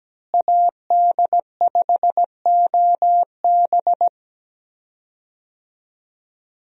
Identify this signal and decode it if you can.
Morse code, heard as a keyed tone, about 17 wpm, 700 Hz. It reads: AD5OB
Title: USA Call Signs - 5 Characters - Repeated 2x - 17wpm